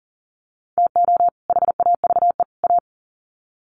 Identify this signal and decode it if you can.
Morse code, heard as a keyed tone, about 40 wpm, 700 Hz. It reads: TO HAVE A
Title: Sets of 3 Words 40wpm